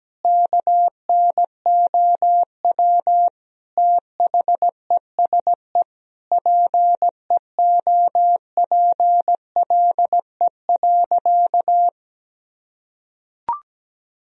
Morse code, heard as a keyed tone, about 17 wpm, 700 Hz. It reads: KNOW THESE PEOPLE.
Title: Sentences from Top 1000 Words - Review Part 1 - 17wpm